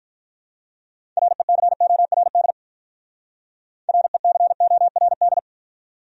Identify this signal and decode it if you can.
Morse code, heard as a keyed tone, about 45 wpm, 700 Hz. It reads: RECORD RECORD